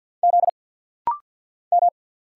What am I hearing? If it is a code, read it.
Morse code, heard as a keyed tone, about 50 wpm, 700 Hz. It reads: Z M
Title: Single Letter-Number Alt - Rapid-Fire 50wpm